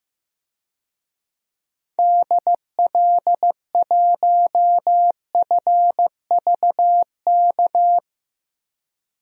Morse code, heard as a keyed tone, about 15 wpm, 700 Hz. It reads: DL1FVK
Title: DXCC-100 Call Signs - 6 Characters - 15wpm